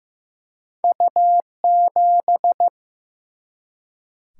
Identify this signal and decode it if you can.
Morse code, heard as a keyed tone, about 15 wpm, 700 Hz. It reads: U 7
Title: Single-Letter Single-Number 15wpm